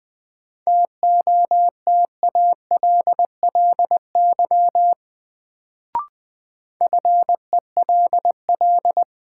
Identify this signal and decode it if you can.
Morse code, heard as a keyed tone, about 20 wpm, 700 Hz. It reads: TOTALLY FELL
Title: Sentences from Top 1600 Words - New Word Drill - 20wpm